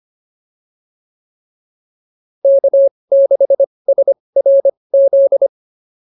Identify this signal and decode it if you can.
Morse code, heard as a keyed tone, about 25 wpm, spaced as for 20 wpm, 550 Hz. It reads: K6SRZ